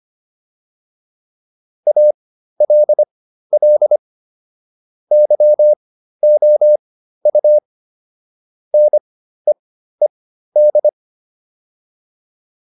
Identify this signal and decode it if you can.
Morse code, heard as a keyed tone, about 25 wpm, spaced as for 13 wpm, 600 Hz. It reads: ALL YOU NEED